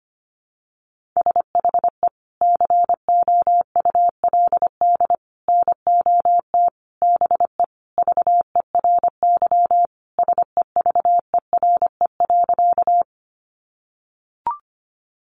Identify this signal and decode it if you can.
Morse code, heard as a keyed tone, about 25 wpm, 700 Hz. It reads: SHE COULD NOT BE VERY SEVERE.
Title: Sentences from Top 1400 Words 25wpm